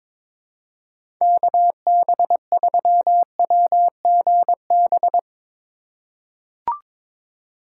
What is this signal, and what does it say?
Morse code, heard as a keyed tone, about 22 wpm, 700 Hz. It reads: KB3WGB
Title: USA Call Signs - 6 Characters - 3x Character Spacing - 22wpm